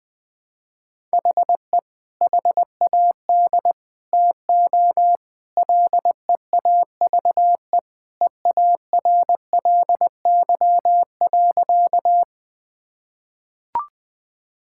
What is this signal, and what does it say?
Morse code, heard as a keyed tone, about 20 wpm, 700 Hz. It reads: HE HAD TO LEAVE EARLY.
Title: Sentences with 3-Word Phrases - 20wpm